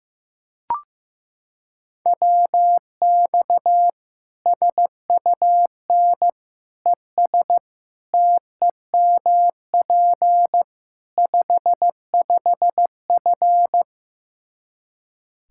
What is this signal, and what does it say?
Morse code, heard as a keyed tone, about 16 wpm, 700 Hz. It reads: WX SUN ES TEMP 55F
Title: K1USN SST QSO Practice 15wpm